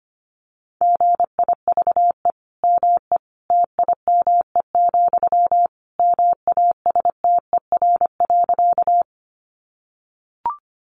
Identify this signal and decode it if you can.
Morse code, heard as a keyed tone, about 25 wpm, 700 Hz. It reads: GIVE ME TIME, MASTER.
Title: Sentences from Top 1300 Words 25wpm